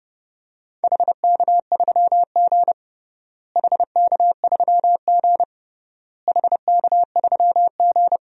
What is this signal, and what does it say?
Morse code, heard as a keyed tone, about 30 wpm, 700 Hz. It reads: HK3G HK3G HK3G
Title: DXCC-100 Call Signs - 4 Characters - Repeated 3x - 30wpm